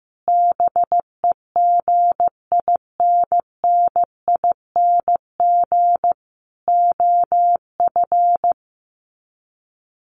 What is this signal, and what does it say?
Morse code, heard as a keyed tone, about 15 wpm, 700 Hz. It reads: BEGINNING OF
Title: Sets of 4 Words 15wpm